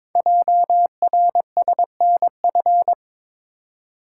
Morse code, heard as a keyed tone, about 22 wpm, 700 Hz. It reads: JRSNF
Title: Code Groups - 5 Letters - 22wpm